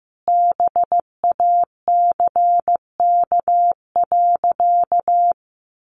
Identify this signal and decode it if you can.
Morse code, heard as a keyed tone, about 15 wpm, 700 Hz. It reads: BACK.
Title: Sentences from Top 300 Words 15wpm